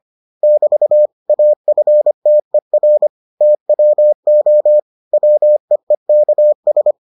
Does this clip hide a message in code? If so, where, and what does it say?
Yes: Morse code from 0.4 to 6.9 s, heard as a keyed tone, about 25 wpm, 600 Hz. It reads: <BT> AFTER TWO WEEKS